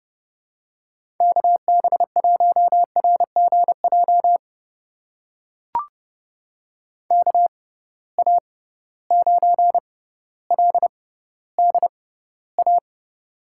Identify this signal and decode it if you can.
Morse code, heard as a keyed tone, about 31 wpm, spaced as for 16 wpm, 700 Hz. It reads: KB1RGJ KA9LDA
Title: USA Call Signs - 6 Characters - 6x Character Spacing - 30wpm